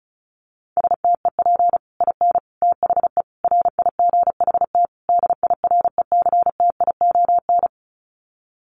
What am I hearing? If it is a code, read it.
Morse code, heard as a keyed tone, about 34 wpm, 700 Hz. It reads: STEP IN THE RIGHT DIRECTION
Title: Sets of 5 Words Encore 35wpm